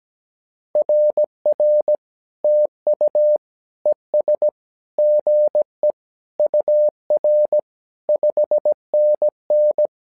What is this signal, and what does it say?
Morse code, heard as a keyed tone, about 17 wpm, 600 Hz. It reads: RR TU ES GE UR 5NN
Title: SOTA QSO Practice 17wpm